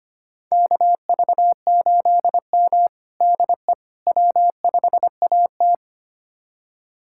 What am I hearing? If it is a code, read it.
Morse code, heard as a keyed tone, about 25 wpm, 700 Hz. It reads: KV8M DE W5AT
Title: USA Callsign DE Callsign - 4 Characters - 25wpm